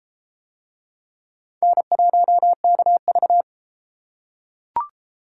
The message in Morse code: N1KV

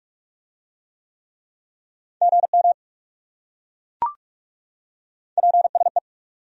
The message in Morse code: GM PSE

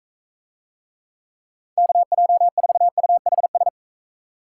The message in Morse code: KJ4UHS

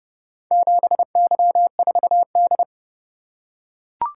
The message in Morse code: 7Y4D

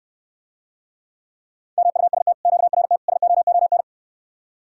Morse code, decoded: MUST QRT SOON

